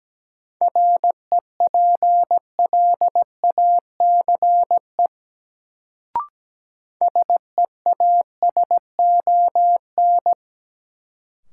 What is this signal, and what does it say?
Morse code, heard as a keyed tone, about 17 wpm, 700 Hz. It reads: REPLACE SEASON
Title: Sentences from Top 1400 Words - New Word Drill - 17wpm